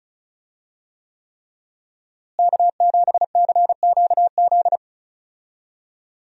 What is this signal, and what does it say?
Morse code, heard as a keyed tone, about 36 wpm, 700 Hz. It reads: K7CQZ
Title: USA Call Signs - 5 Characters - Repeated 3x - 35wpm